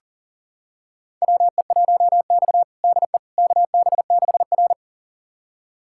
Morse code, heard as a keyed tone, about 40 wpm, 700 Hz. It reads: WE1X DE KB6R
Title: USA Callsign DE Callsign - 4 Characters - 40wpm